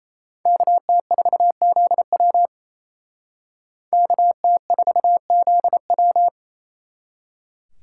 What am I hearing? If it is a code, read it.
Morse code, heard as a keyed tone, about 36 wpm, 700 Hz. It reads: KT4ZW KT4ZW